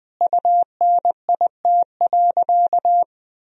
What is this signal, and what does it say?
Morse code, heard as a keyed tone, about 20 wpm, 700 Hz. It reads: UNIT.
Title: Sentences from Top 1000 Words - Review Part 2 - 20wpm